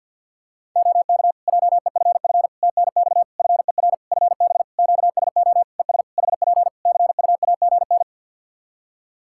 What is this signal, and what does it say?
Morse code, heard as a keyed tone, about 50 wpm, 700 Hz. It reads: OK JEFF TNX FER FB QSO ES HP CUAGN
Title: Practice QSO - Exchange 4 - Ending - 50wpm